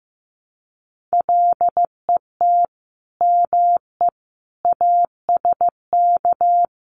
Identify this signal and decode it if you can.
Morse code, heard as a keyed tone, about 15 wpm, 700 Hz. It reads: LET ME ASK
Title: Sets of 5 Words Encore 15wpm